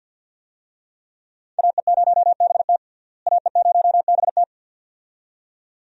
Morse code, heard as a keyed tone, about 46 wpm, 700 Hz. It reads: AE0BT AE0BT
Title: USA Call Signs - 5 Characters - Repeated 2x - 50wpm